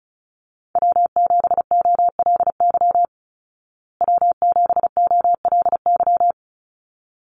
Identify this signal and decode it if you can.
Morse code, heard as a keyed tone, about 34 wpm, 700 Hz. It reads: W7OLY W7OLY
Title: Call Signs Repeated 3x - 35wpm